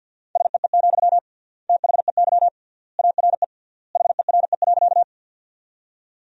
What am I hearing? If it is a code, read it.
Morse code, heard as a keyed tone, about 50 wpm, 700 Hz. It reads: SEE, THEY ARE HERE.
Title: 3x Word-Spacing - Sentences from Top 200 Words - 50wpm